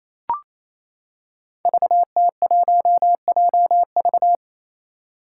Morse code, heard as a keyed tone, about 28 wpm, 700 Hz. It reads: VT1JV
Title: Five Challenging Letter-Number Combinations 28wpm